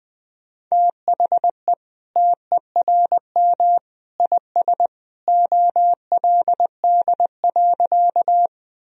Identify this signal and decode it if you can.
Morse code, heard as a keyed tone, about 20 wpm, 700 Hz. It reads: THE TERM IS OLD.
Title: Sentences from Top 1000 Words - Review Part 4 - 20wpm